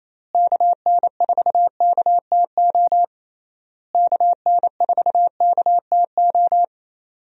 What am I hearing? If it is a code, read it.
Morse code, heard as a keyed tone, about 28 wpm, 700 Hz. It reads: KN4KTO KN4KTO
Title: USA Call Signs - 6 Characters - Repeated 3x - 28wpm